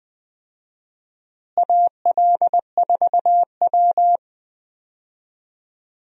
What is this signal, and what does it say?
Morse code, heard as a keyed tone, about 20 wpm, 700 Hz. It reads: AL4W
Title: USA Call Signs - 4 Characters - Repeated 2x - 20wpm